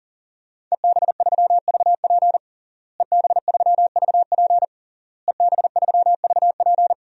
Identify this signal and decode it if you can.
Morse code, heard as a keyed tone, about 40 wpm, 700 Hz. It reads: EB3VP EB3VP EB3VP
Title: DXCC-100 Call Signs - 5 Characters - Repeated 3x - 40wpm